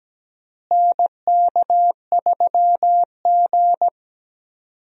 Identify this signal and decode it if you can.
Morse code, heard as a keyed tone, about 17 wpm, 700 Hz. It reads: NK3G
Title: USA Call Signs - 4 Characters - Repeated 2x - 17wpm